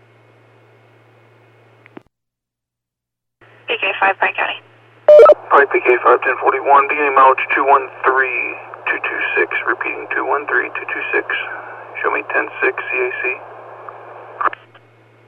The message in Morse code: none

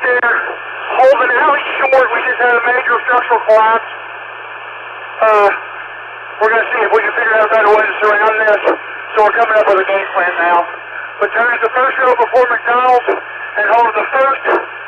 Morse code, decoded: none

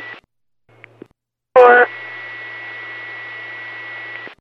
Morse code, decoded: none

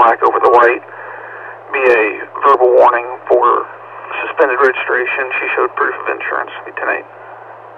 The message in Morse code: none

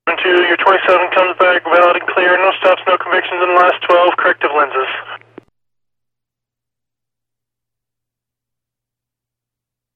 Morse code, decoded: none

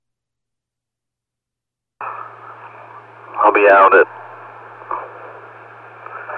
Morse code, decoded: none

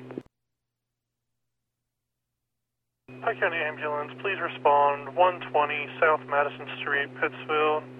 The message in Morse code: none

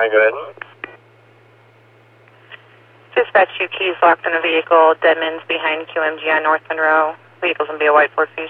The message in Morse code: none

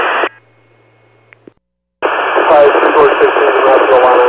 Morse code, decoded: none

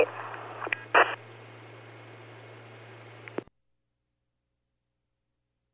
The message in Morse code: none

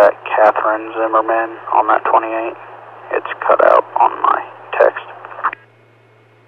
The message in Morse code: none

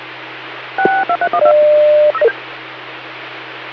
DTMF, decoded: 6231

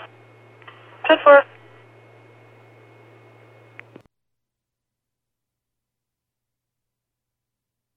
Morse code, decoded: none

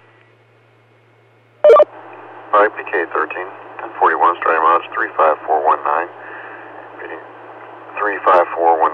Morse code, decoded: none

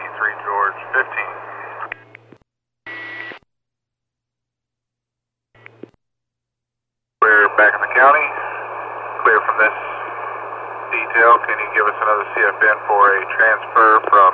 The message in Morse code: none

